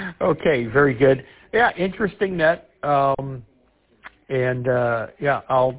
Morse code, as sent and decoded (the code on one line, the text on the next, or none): none